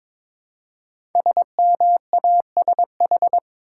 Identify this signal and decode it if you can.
Morse code, heard as a keyed tone, about 22 wpm, 700 Hz. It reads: SMASH
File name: 5 Letter Words - First 3 Letters Given - 22wpm